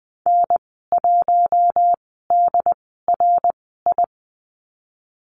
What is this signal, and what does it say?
Morse code, heard as a keyed tone, about 20 wpm, spaced as for 14 wpm, 700 Hz. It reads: N1DRI
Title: 154 Lesson 25g - Letter p - Call Signs 20wpm